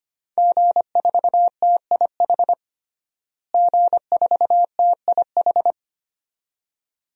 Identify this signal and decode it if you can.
Morse code, heard as a keyed tone, about 25 wpm, 700 Hz. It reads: G4TIH G4TIH
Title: DXCC-100 Call Signs - 5 Characters - Repeated 2x - 25wpm